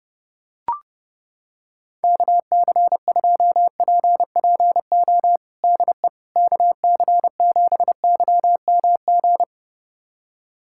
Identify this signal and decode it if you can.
Morse code, heard as a keyed tone, about 30 wpm, 700 Hz. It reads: KC2PPO DE KC7YMG
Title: USA Callsign DE Callsign - 6 Characters - 30wpm